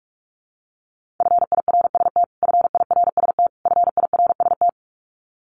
Morse code, heard as a keyed tone, about 44 wpm, 700 Hz. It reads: FIRST FIRST FIRST
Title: Top 200 Words Repeated 45wpm